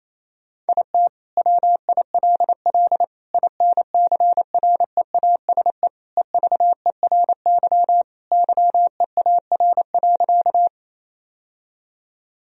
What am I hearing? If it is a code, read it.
Morse code, heard as a keyed tone, about 28 wpm, 700 Hz. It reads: IT WILL INCREASE EVERY YEAR.
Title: Sentences from Top 1000 Words - Review Part 4 - 28wpm